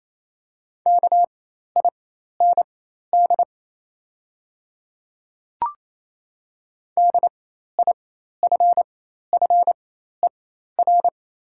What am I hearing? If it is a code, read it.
Morse code, heard as a keyed tone, about 27 wpm, spaced as for 13 wpm, 700 Hz. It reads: KIND DIFFER
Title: 4x Character Spacing - Top 300 Words - 28wpm